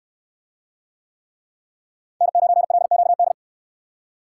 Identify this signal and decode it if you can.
Morse code, heard as a keyed tone, about 68 wpm, 700 Hz. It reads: N9LQD